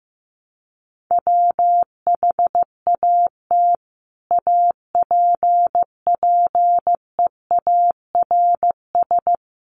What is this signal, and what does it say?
Morse code, heard as a keyed tone, about 15 wpm, 700 Hz. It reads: WHAT APPEARS
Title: Sets of 5 Words Encore2 15wpm